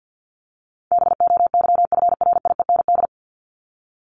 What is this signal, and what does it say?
Morse code, heard as a keyed tone, about 50 wpm, 700 Hz. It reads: BOYFRIEND
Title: Top 501-750 Words 50wpm